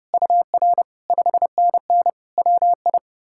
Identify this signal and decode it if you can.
Morse code, heard as a keyed tone, about 37 wpm, 700 Hz. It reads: UR 5NN WI
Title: Speed-Racing Common QSO Phrases - 30wpm